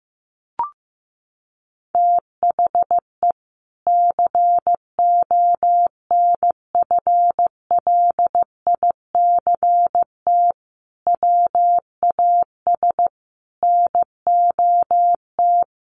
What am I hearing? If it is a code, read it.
Morse code, heard as a keyed tone, about 15 wpm, 700 Hz. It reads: THE CONFLICT WAS NOT
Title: Sentences from Top 1300 Words 15wpm